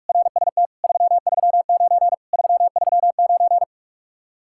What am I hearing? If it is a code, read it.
Morse code, heard as a keyed tone, about 59 wpm, 700 Hz. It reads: RST 339 339